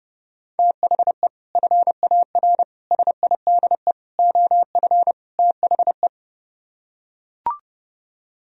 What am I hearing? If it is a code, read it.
Morse code, heard as a keyed tone, about 30 wpm, 700 Hz. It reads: THE FAR SIDE OF THE